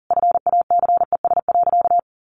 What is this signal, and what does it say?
Morse code, heard as a keyed tone, about 40 wpm, 700 Hz. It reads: FACES.
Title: A Sound Of Thunder 40wpm